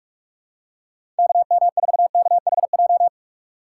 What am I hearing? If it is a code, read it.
Morse code, heard as a keyed tone, about 43 wpm, 700 Hz. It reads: KM4KHJ
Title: USA Call Signs - 6x Character Spacing - 45wpm